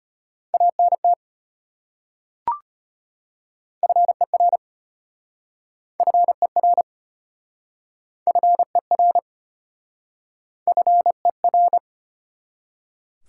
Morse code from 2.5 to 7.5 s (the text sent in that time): FER FER